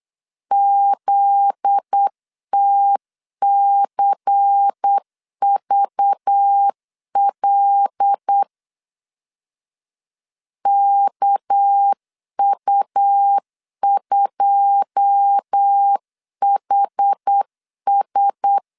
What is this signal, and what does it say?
Morse code, heard as a keyed tone, about 8 wpm, 800 Hz. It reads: ZTCVL KU2HS
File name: CW / DET / 1 9 1500C S A 8wpm 5637